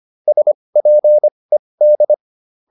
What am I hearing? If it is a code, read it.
Morse code, heard as a keyed tone, about 25 wpm, spaced as for 20 wpm, 600 Hz. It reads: SPED